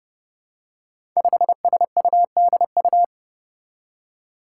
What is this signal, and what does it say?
Morse code, heard as a keyed tone, about 32 wpm, 700 Hz. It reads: 5SUDU
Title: Five Challenging Letter-Number Combinations 30wpm